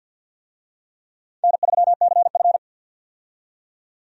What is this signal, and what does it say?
Morse code, heard as a keyed tone, about 51 wpm, 700 Hz. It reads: N3CF
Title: USA Call Signs - 4 Characters - 50wpm